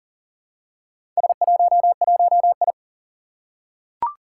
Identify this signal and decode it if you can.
Morse code, heard as a keyed tone, about 40 wpm, 690 Hz. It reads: S11I
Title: Four Challenging Letter-Number Combinations 40wpm